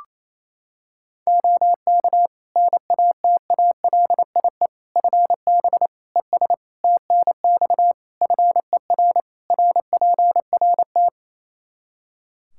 Morse code, heard as a keyed tone, about 28 wpm, 700 Hz. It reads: OK NATALIE FB ES TNX FER RPRT
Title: Practice QSO - Exchange 2 - Info - 28wpm